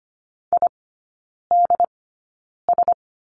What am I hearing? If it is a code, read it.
Morse code, heard as a keyed tone, about 25 wpm, spaced as for 9 wpm, 690 Hz. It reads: IDS
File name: Lesson 14e - Number 2 - Three Characters 25wpm